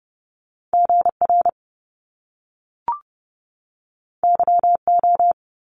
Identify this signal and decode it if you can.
Morse code, heard as a keyed tone, about 30 wpm, 700 Hz. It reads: GR YO
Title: Top Two Letter Combinations in Words 30wpm